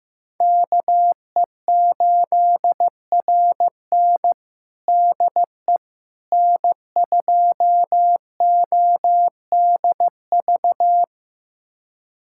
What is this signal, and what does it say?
Morse code, heard as a keyed tone, about 15 wpm, 700 Hz. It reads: KE8RN DE N2ODV
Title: USA Callsign DE Callsign - 5 Characters - 15wpm